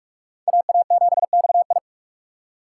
Morse code, heard as a keyed tone, about 45 wpm, 700 Hz. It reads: AA7XI